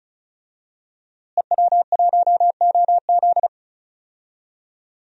Morse code, heard as a keyed tone, about 34 wpm, 700 Hz. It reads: EW1OZ
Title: DXCC-100 Call Signs - 5 Characters - Repeated 3x - 35wpm